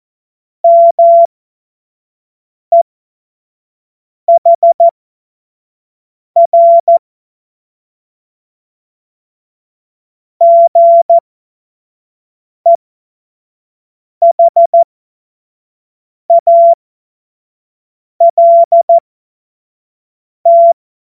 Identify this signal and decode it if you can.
Morse code, heard as a keyed tone, about 14 wpm, spaced as for 5 wpm, 680 Hz. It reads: MEHR GEHALT
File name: Morsekurs / Witze / Witz 10. Januar 2023 25 F70